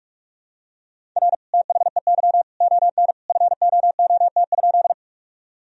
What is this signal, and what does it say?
Morse code, heard as a keyed tone, about 46 wpm, 700 Hz. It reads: R THEY ON FOOT?
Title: Sentences from Top 500 Words and Abbreviations 45wpm